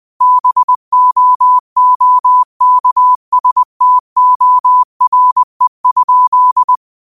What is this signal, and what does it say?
Morse code, heard as a keyed tone, about 20 wpm, 1000 Hz. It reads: BOOKSTORE?